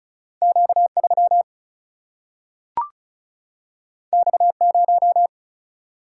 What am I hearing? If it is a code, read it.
Morse code, heard as a keyed tone, about 36 wpm, 700 Hz. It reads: Q 3 X 0
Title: Single-Letter Single-Number 35wpm